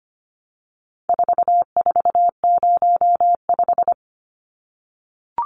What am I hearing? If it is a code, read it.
Morse code, heard as a keyed tone, about 25 wpm, 700 Hz. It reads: 4405
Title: Numbers - 4-Digit 25wpm